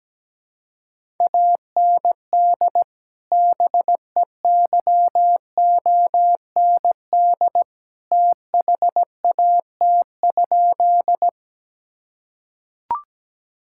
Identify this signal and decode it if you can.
Morse code, heard as a keyed tone, about 17 wpm, 700 Hz. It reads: AND BEYOND THAT?
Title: Sentences from Top 1000 Words - Review Part 6 - 17wpm